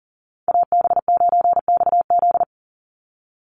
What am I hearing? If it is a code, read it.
Morse code, heard as a keyed tone, about 40 wpm, 700 Hz. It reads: AB9XZ
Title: USA Call Signs 40wpm